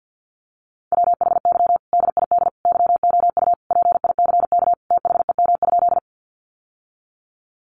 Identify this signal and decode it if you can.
Morse code, heard as a keyed tone, about 50 wpm, 700 Hz. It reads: WHY DID YOU PICK THEM?
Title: Sentences from Top 1000 Words 50wpm